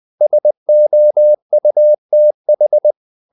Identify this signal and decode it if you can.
Morse code, heard as a keyed tone, about 20 wpm, 600 Hz. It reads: SOUTH
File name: CW / Practice / NationalCapitals20